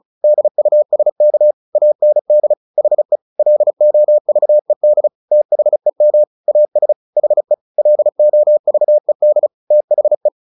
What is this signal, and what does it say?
Morse code, heard as a keyed tone, about 34 wpm, 600 Hz. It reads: DUSK AND HE LOVED THEM AS HE LOVED THE
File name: the old man and the sea / the old man and the sea 1 35wpm